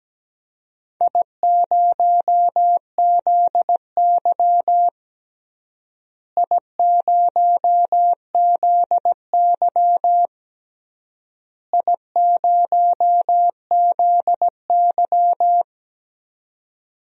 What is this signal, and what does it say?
Morse code, heard as a keyed tone, about 17 wpm, 700 Hz. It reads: I0ZY I0ZY I0ZY
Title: DXCC-100 Call Signs - 4 Characters - Repeated 3x - 17wpm